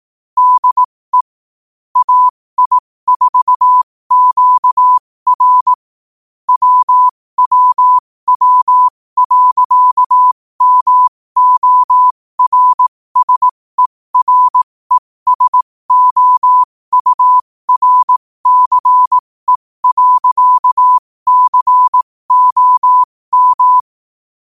Morse code, heard as a keyed tone, about 18 wpm, spaced as for 15 wpm, 1000 Hz. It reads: DE AI4QR WWW.MORSERESOURCE.COM
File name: 15wpm